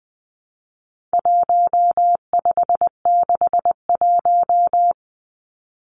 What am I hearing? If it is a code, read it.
Morse code, heard as a keyed tone, about 20 wpm, 700 Hz. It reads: 1561
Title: Numbers - 4-Digit 20wpm